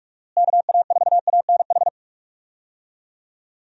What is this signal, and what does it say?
Morse code, heard as a keyed tone, about 44 wpm, 700 Hz. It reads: KA4ANH
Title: USA Call Signs - 6 Characters - Repeated 2x - 45wpm